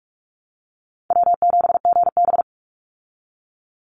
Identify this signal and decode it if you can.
Morse code, heard as a keyed tone, about 44 wpm, 700 Hz. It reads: W7GB
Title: Easy Call Signs Repeated 3x - 45wpm